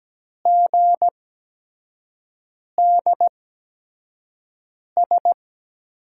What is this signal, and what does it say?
Morse code, heard as a keyed tone, about 17 wpm, spaced as for 5 wpm, 700 Hz. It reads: GDS
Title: USA Call Signs - 8x Character Spacing - 17wpm